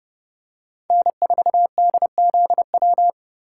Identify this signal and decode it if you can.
Morse code, heard as a keyed tone, about 30 wpm, 700 Hz. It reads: N4DZW
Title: USA Call Signs - 5 Characters - 3x Character Spacing - 30wpm